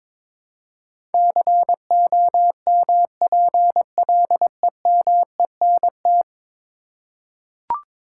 Morse code, heard as a keyed tone, about 22 wpm, 700 Hz. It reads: COMPLEMENT
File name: practice / 22wpm / 10 Letter Words - First 3 Letters Given - 22wpm